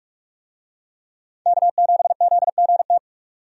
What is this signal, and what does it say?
Morse code, heard as a keyed tone, about 44 wpm, 700 Hz. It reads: K7ZGT
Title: USA Call Signs - 5 Characters - 45wpm